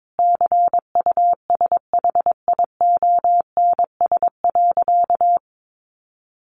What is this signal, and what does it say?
Morse code, heard as a keyed tone, about 22 wpm, 700 Hz. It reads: CUSHIONS.
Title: A Sound Of Thunder 22wpm